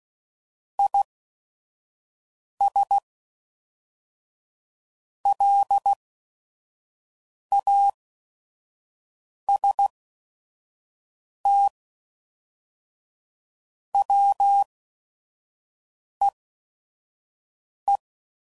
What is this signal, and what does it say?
Morse code, heard as a keyed tone, about 16 wpm, spaced as for 5 wpm, 800 Hz. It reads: IS LAST WEE
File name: CW Code Groups @5WPM / QSOs / QSO WD0FBT